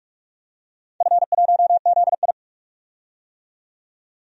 Morse code, heard as a keyed tone, about 44 wpm, 700 Hz. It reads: F1ZI
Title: DXCC-100 Call Signs - 4 Characters - 45wpm